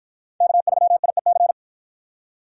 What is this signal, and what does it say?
Morse code, heard as a keyed tone, about 50 wpm, 700 Hz. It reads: K3IEC